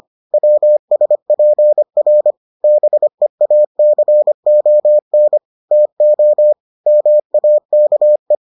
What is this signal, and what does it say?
Morse code, heard as a keyed tone, about 25 wpm, 600 Hz. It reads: WSPR BEACON TO MAKE